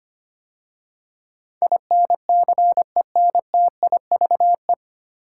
Morse code, heard as a keyed tone, about 25 wpm, 700 Hz. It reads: INCENTIVE
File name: practice / 25wpm / 9 Letter Words - First 3 Letters Given - 25wpm